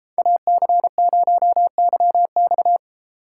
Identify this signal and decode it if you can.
Morse code, heard as a keyed tone, about 34 wpm, 700 Hz. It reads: AC0YX
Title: Speed Racing - USA 2x2 Call Signs 28wpm